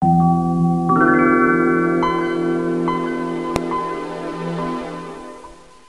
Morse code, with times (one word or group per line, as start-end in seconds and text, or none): none